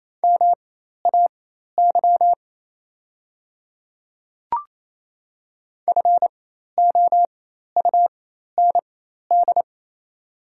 Morse code, heard as a keyed tone, about 27 wpm, spaced as for 13 wpm, 700 Hz. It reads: MAY FOUND
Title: 4x Character Spacing - Top 300 Words - 28wpm